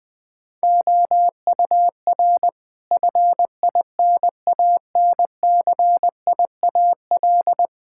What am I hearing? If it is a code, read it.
Morse code, heard as a keyed tone, about 20 wpm, 700 Hz. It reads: OUR FINANCIAL
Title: Sentences from Top 1000 Words - Review Part 5 - 20wpm